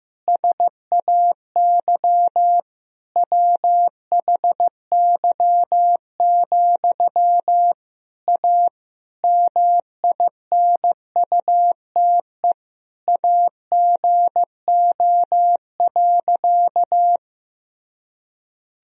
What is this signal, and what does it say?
Morse code, heard as a keyed tone, about 15 wpm, 700 Hz. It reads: SAY WHY, A MINUTE AGO.
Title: Sentences from Top 500 Words and Abbreviations 15wpm